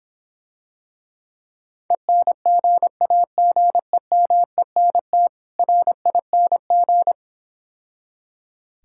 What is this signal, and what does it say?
Morse code, heard as a keyed tone, about 26 wpm, 700 Hz. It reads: ENGAGEMENT RING